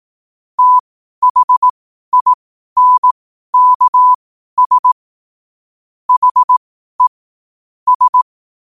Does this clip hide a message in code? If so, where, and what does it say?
Yes: Morse code from 0.6 to 8.2 s, heard as a keyed tone, about 18 wpm, spaced as for 12 wpm, 1000 Hz. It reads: THINKS HE S